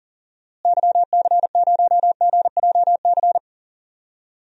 Morse code, heard as a keyed tone, about 40 wpm, 700 Hz. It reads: YC0GJC
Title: DXCC-100 Call Signs - 6 Characters - Repeated 3x - 40wpm